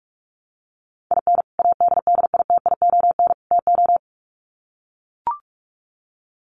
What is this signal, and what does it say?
Morse code, heard as a keyed tone, about 44 wpm, 700 Hz. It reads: IN ADDITION TO
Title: Sets of 3 Words 45wpm